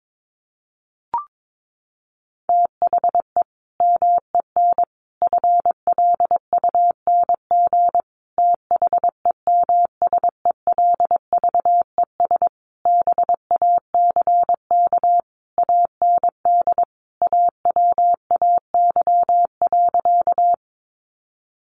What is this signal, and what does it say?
Morse code, heard as a keyed tone, about 22 wpm, 700 Hz. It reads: THE MEN FLUNG THEMSELVES BACK AND AWAY.
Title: A Sound Of Thunder 22wpm